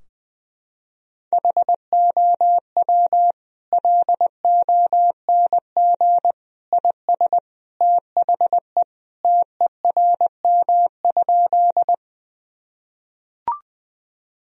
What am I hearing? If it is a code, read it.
Morse code, heard as a keyed tone, about 20 wpm, 700 Hz. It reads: HOW LONG IS THE TERM?
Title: Sentences from Top 1000 Words - Review Part 4 - 20wpm